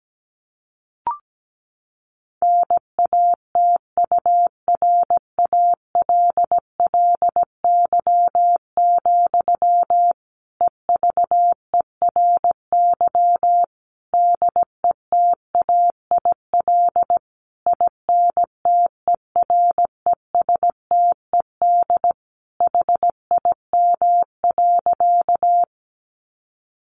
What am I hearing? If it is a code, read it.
Morse code, heard as a keyed tone, about 17 wpm, 700 Hz. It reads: NATURALLY, EVERY DETAIL INTERESTED HIM.
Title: Sentences from Top 1500 Words 17wpm